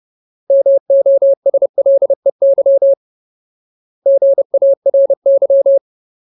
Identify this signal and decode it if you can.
Morse code, heard as a keyed tone, about 30 wpm, 550 Hz. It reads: MOSLEY GARY
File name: Ultimate Word List / 30WPM Ultimate